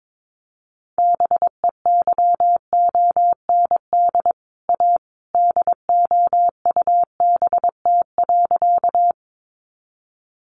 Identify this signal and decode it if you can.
Morse code, heard as a keyed tone, about 22 wpm, 700 Hz. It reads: BEYOND A DOUBT.
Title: Sentences from Top 1500 Words 22wpm